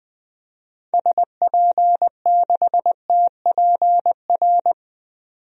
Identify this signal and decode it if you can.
Morse code, heard as a keyed tone, about 21 wpm, 700 Hz. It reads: SP6TPR
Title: DXCC-100 Call Signs - 6 Characters - 20wpm